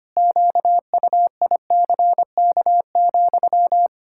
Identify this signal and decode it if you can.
Morse code, heard as a keyed tone, about 25 wpm, 700 Hz. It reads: QUICK,
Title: Pangram Challenge v2 - 25wpm